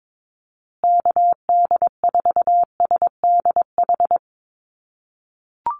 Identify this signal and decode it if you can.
Morse code, heard as a keyed tone, about 22 wpm, 700 Hz. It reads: KD4SDH
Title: USA Call Signs Repeated 2x 22wpm